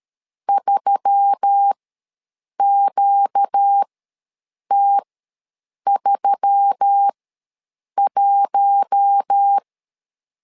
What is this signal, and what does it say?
Morse code, heard as a keyed tone, about 13 wpm, spaced as for 7 wpm, 800 Hz. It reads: 3QT31
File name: LEZIONI / 4 Q1S3 ws 12wpm 2038